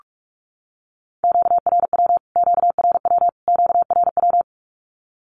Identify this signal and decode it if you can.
Morse code, heard as a keyed tone, about 44 wpm, 700 Hz. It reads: QRW QRW QRW
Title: Q-Codes Repeated 3x 45wpm